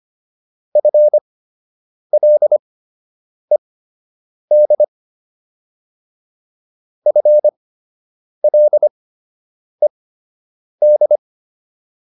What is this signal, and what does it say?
Morse code, heard as a keyed tone, about 25 wpm, spaced as for 8 wpm, 600 Hz. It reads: FLED FLED
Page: fundamental-practice-files